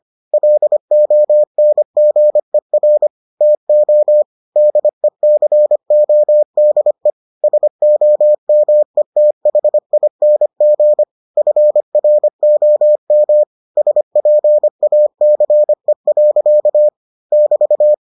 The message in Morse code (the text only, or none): LONGER TO DECODE SOMETHING FROM SPACE. <BT>